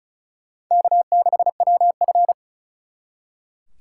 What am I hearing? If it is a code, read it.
Morse code, heard as a keyed tone, about 44 wpm, 700 Hz. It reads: K6WF